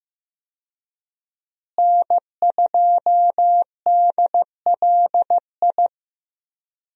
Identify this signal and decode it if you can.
Morse code, heard as a keyed tone, about 15 wpm, 700 Hz. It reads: N2DLI
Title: USA Call Signs - 5 Characters - Repeated 2x - 15wpm